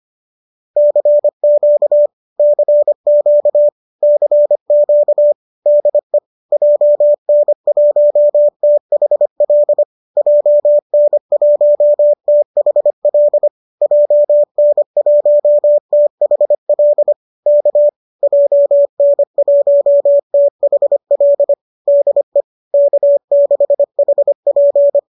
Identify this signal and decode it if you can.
Morse code, heard as a keyed tone, about 25 wpm, 590 Hz. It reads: CQ CQ CQ DE JN1THL JN1THL JN1THL K JN1THL DE K6HP
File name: qso205 25